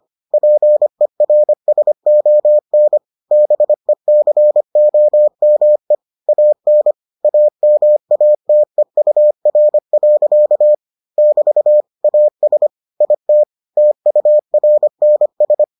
Morse code, heard as a keyed tone, about 25 wpm, 600 Hz. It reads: PERSON BECOME AN AMATEUR. <BT> AS IT TURNS